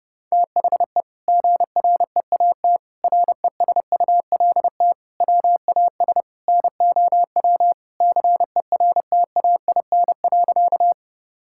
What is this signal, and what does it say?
Morse code, heard as a keyed tone, about 28 wpm, 700 Hz. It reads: THE GREAT RESULT WAS NOW CERTAIN.
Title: Sentences from Top 1000 Words - Review Part 5 - 30wpm